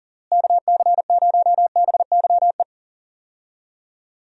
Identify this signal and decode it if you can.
Morse code, heard as a keyed tone, about 40 wpm, 700 Hz. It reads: KC0BYE
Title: USA Call Signs - 6 Characters - Repeated 2x - 40wpm